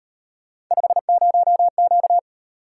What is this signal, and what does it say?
Morse code, heard as a keyed tone, about 37 wpm, 700 Hz. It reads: 50Q